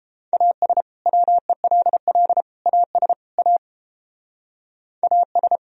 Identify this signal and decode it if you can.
Morse code, heard as a keyed tone, about 29 wpm, 700 Hz. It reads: AS WELL AS A AS